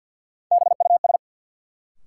Morse code, heard as a keyed tone, about 49 wpm, 700 Hz. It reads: BUS
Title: Sentences from Top 1000 Words - Review Part 6 - New Word Drill - 50wpm